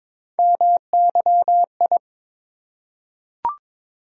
Morse code, heard as a keyed tone, about 22 wpm, 700 Hz. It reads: MYI
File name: practice / 22wpm / Three Letter-Number Combinations - 22wpm